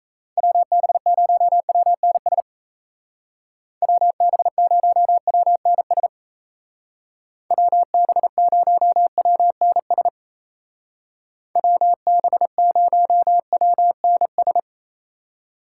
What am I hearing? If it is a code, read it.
Morse code, heard as a keyed tone, about 38 wpm, 700 Hz. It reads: WB0WNS WB0WNS WB0WNS WB0WNS